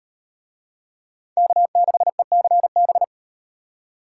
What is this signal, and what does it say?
Morse code, heard as a keyed tone, about 38 wpm, 700 Hz. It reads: K6ECB